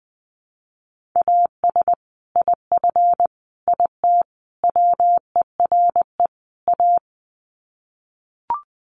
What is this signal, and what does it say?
Morse code, heard as a keyed tone, about 20 wpm, 700 Hz. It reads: AS IF IT WERE A
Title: Sets of 5 Words 20wpm